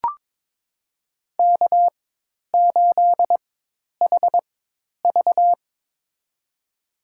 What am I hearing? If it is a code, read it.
Morse code, heard as a keyed tone, about 22 wpm, spaced as for 10 wpm, 700 Hz. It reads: K8HV
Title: USA Call Signs - 4 Characters - 4x Character Spacing - 22wpm